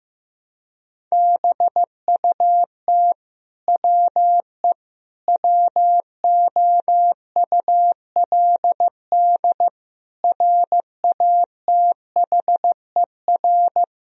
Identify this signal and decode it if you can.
Morse code, heard as a keyed tone, about 16 wpm, 700 Hz. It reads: BUT WE WOULD RATHER
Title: Sentences from Top 1000 Words - Review Part 4 - 15wpm